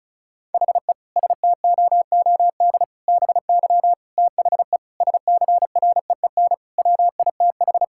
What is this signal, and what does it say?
Morse code, heard as a keyed tone, about 36 wpm, 700 Hz. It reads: HE STOOD BY THE SCREEN WITH
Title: Sentences from Top 1000 Words - Review Part 6 - 35wpm